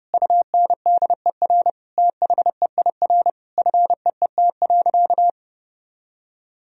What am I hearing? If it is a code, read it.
Morse code, heard as a keyed tone, about 32 wpm, 700 Hz. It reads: UNDER THEIR FEET.
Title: Sentences from Top 1000 Words - Review Part 4 - 30wpm